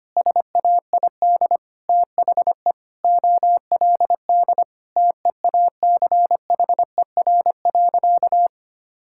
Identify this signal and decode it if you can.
Morse code, heard as a keyed tone, about 25 wpm, 700 Hz. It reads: SAID THE OLD TEACHER.
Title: Sentences from Top 1000 Words - Review Part 6 - 25wpm